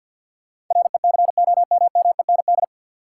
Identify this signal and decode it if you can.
Morse code, heard as a keyed tone, about 51 wpm, 700 Hz. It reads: RECOMMEND
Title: Sentences from Top 1200 Words - New Word Drill Alt - 50wpm